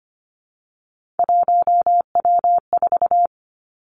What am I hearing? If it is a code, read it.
Morse code, heard as a keyed tone, about 25 wpm, 700 Hz. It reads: 1W4
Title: Three Challenging Letter-Number Combinations 25wpm